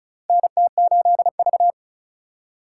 Morse code, heard as a keyed tone, about 34 wpm, 700 Hz. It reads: NT8V
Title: USA Call Signs - 4 Characters - 4x Character Spacing - 35wpm